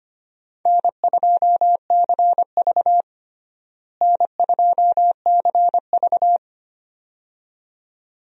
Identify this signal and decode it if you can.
Morse code, heard as a keyed tone, about 25 wpm, 700 Hz. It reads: N2CV N2CV
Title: USA Call Signs - 4 Characters - Repeated 2x - 25wpm